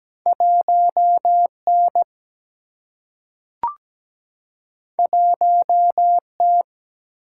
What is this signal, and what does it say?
Morse code, heard as a keyed tone, about 17 wpm, 700 Hz. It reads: 1 N 1 T
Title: Single-Letter Single-Number 17wpm